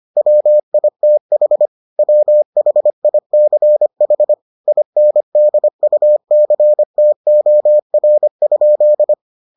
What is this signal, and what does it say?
Morse code, heard as a keyed tone, about 25 wpm, 600 Hz. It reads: WITH WHICH INDUCTOR?